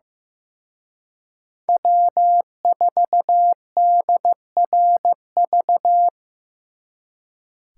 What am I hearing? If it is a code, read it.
Morse code, heard as a keyed tone, about 18 wpm, 700 Hz. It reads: W4DRV
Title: Speed Racing - USA 1x3 Call Signs 15wpm